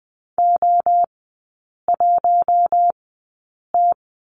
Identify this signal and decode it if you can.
Morse code, heard as a keyed tone, about 20 wpm, spaced as for 8 wpm, 700 Hz. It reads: O1T
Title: 074 Lesson 13e - Letter L - Three Characters 20wpm